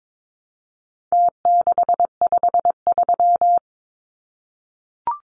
Morse code, heard as a keyed tone, about 22 wpm, 700 Hz. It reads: T653
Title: Numbers - Contest Style - 4-Digit 22wpm